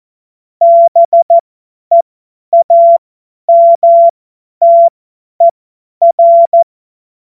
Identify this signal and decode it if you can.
Morse code, heard as a keyed tone, about 14 wpm, spaced as for 10 wpm, 680 Hz. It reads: BEAMTER
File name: Witz-12-August-2025-50-F70